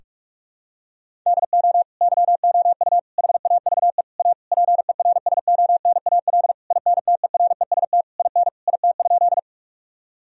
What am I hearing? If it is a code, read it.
Morse code, heard as a keyed tone, about 41 wpm, 700 Hz. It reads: DO YOU HAVE A PERSONAL INTEREST IN IT?
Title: Sentences from Top 1000 Words - Review Part 4 - 45wpm